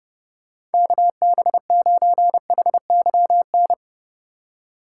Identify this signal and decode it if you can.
Morse code, heard as a keyed tone, about 30 wpm, 700 Hz. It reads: KB9HYN
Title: USA Call Signs - 6 Characters - 4x Character Spacing - 30wpm